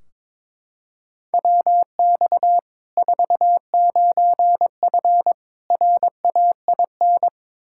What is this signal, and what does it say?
Morse code, heard as a keyed tone, about 22 wpm, 700 Hz. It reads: WX 49F RAIN
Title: Practice QSO - Exchange 2 - Info - 22wpm